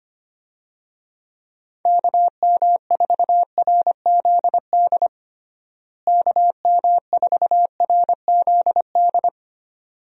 Morse code, heard as a keyed tone, about 25 wpm, 700 Hz. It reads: KM4RZD KM4RZD
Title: USA Call Signs - 6 Characters - Repeated 2x - 25wpm